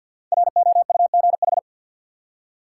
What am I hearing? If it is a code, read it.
Morse code, heard as a keyed tone, about 52 wpm, 700 Hz. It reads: ROUGH
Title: Five Letter Words - 50wpm